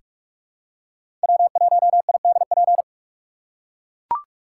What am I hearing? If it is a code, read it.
Morse code, heard as a keyed tone, about 43 wpm, 700 Hz. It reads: W1IDP
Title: USA Call Signs - 5 Characters - 2x Character Spacing - 45wpm